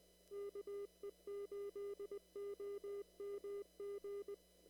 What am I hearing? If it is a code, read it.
Morse code, heard as a keyed tone, about 19 wpm, 400 Hz. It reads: KE8OMG
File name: CW / kt5j wpx cw-op-k5pi